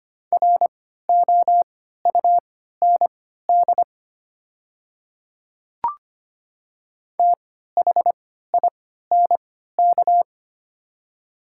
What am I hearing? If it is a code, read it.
Morse code, heard as a keyed tone, about 25 wpm, spaced as for 14 wpm, 700 Hz. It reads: ROUND THINK
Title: 3x Character Spacing - Top 200 Words - 25wpm